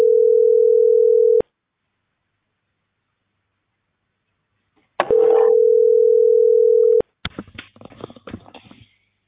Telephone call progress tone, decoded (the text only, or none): ringback tone